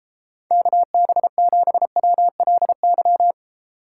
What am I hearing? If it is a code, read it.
Morse code, heard as a keyed tone, about 33 wpm, 700 Hz. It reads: KB7WLY